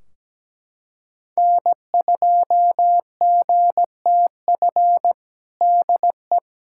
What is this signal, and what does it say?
Morse code, heard as a keyed tone, about 17 wpm, 700 Hz. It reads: N2GTF DE
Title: Practice QSO - Exchange 4 - Ending - 17wpm